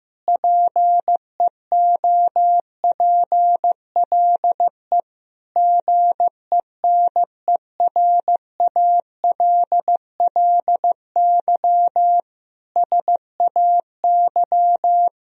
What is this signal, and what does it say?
Morse code, heard as a keyed tone, about 16 wpm, 700 Hz. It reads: PEOPLE GENERALLY SAY
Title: Sentences from Top 1000 Words - Review Part 4 - 15wpm